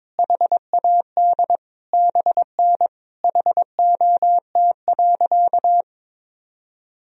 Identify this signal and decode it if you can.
Morse code, heard as a keyed tone, about 22 wpm, 700 Hz. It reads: HAD BN HOT.
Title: Sentences from Top 500 Words and Abbreviations 22wpm